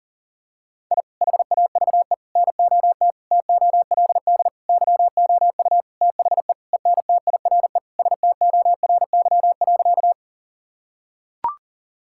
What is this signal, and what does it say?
Morse code, heard as a keyed tone, about 40 wpm, 700 Hz. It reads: I HAVE NOT TOLD YOU THE ENTIRE STORY.
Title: Sentences from Top 1000 Words - Review Part 5 - 40wpm